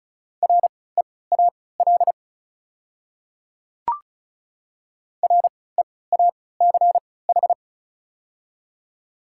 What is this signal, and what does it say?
Morse code, heard as a keyed tone, about 36 wpm, spaced as for 20 wpm, 700 Hz. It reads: REAL REACH
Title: 3x Character Spacing - Top 500 Words - 35wpm